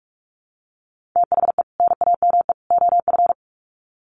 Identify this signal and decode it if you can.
Morse code, heard as a keyed tone, about 44 wpm, 700 Hz. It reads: THE NAME OF
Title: Sets of 3 Words 45wpm